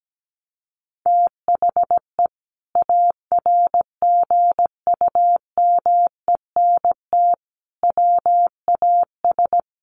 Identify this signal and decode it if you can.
Morse code, heard as a keyed tone, about 17 wpm, 700 Hz. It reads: THE ARGUMENT WAS
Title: Sentences from Top 1400 Words 17wpm